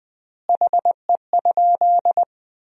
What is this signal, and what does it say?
Morse code, heard as a keyed tone, about 20 wpm, 700 Hz. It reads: HE?
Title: 3x Word-Spacing - Sentences from Top 500 Words - 20wpm